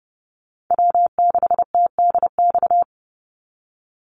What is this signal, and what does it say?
Morse code, heard as a keyed tone, about 29 wpm, 700 Hz. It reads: W6TDX
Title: NA SOTA Chaser Callsigns 30wpm